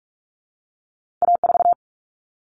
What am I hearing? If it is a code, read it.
Morse code, heard as a keyed tone, about 46 wpm, 700 Hz. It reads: A4
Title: Common Cut Numbers - 2-Digit 45wpm